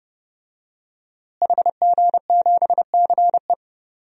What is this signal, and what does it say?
Morse code, heard as a keyed tone, about 30 wpm, 700 Hz. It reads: HG7CE
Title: DXCC-100 Call Signs - 5 Characters - Repeated 3x - 30wpm